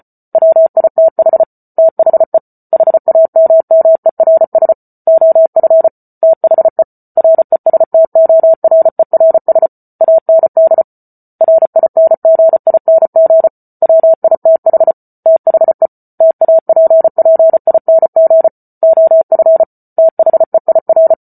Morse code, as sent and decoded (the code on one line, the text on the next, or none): .-- .. - .... / - .... / .... .- -- -- . .-. ... / --- ..-. / - .... / .-. ... - --- .-. .-. ... / .- -. -.. / .-. .. -. --. .. -. --. / .-- .. - .... / - .... / - .- .--. .--. .. -. --. / --- ..-. / - .... .. .-.
WITH THE HAMMERS OF THE RESTORERS AND RINGING WITH THE TAPPING OF THEIR